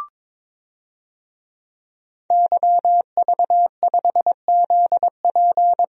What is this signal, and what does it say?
Morse code, heard as a keyed tone, about 22 wpm, 700 Hz. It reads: YV5ZP
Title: DXCC-100 Call Signs - 5 Characters - 22wpm